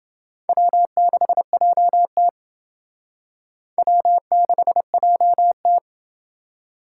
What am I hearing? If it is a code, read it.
Morse code, heard as a keyed tone, about 29 wpm, 700 Hz. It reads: W6JT W6JT